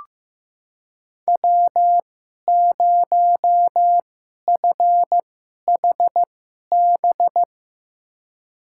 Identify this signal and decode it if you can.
Morse code, heard as a keyed tone, about 15 wpm, spaced as for 11 wpm, 700 Hz. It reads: W0FHB
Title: USA Call Signs - 2x Character Spacing - 15wpm